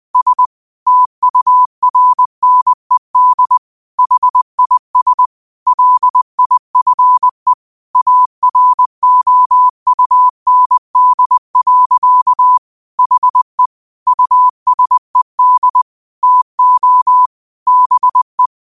Morse code, heard as a keyed tone, about 20 wpm, 1000 Hz. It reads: S TURNED HIS LIFE AROUND. HE USED TO BE
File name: qotdfiles / 20wpm